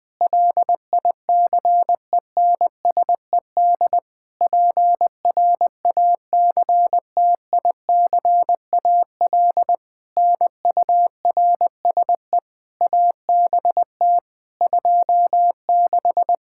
Morse code, heard as a keyed tone, about 20 wpm, 700 Hz. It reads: LICENSED PRACTICAL NURSE ABT 26
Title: Practice QSO - Exchange 3 - More Info - 20wpm